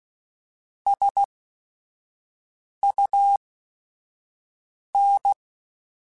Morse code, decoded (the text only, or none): SUN